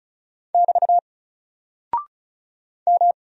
Morse code, heard as a keyed tone, about 36 wpm, 700 Hz. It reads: <BT> M
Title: Single Character (Letter:Number:Punctuation:Prosign) Alt 35wpm